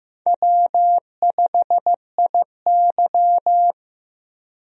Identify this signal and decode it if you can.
Morse code, heard as a keyed tone, about 15 wpm, 700 Hz. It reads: W5IY
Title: USA Call Signs - 4 Characters - 15wpm